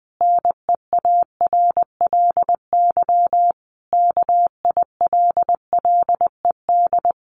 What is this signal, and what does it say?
Morse code, heard as a keyed tone, about 20 wpm, 700 Hz. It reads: NEARLY KILLED
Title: A Sound Of Thunder 20wpm